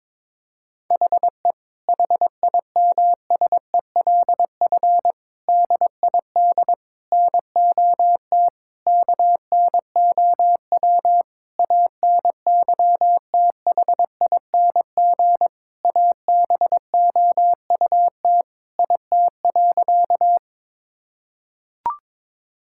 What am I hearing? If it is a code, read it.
Morse code, heard as a keyed tone, about 22 wpm, 700 Hz. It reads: HE HIMSELF DID NOT KNOW ANYTHING ABOUT IT.
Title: Sentences from Top 1600 Words 22wpm